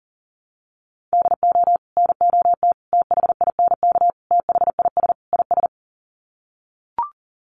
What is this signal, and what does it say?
Morse code, heard as a keyed tone, about 40 wpm, 700 Hz. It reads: DO NOT THINK THIS IS
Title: Sets of 5 Words Encore2 40wpm